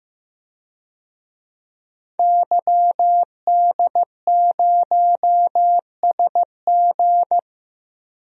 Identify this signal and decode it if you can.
Morse code, heard as a keyed tone, about 15 wpm, 700 Hz. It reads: YD0SG
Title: DXCC-100 Call Signs - 5 Characters - 15wpm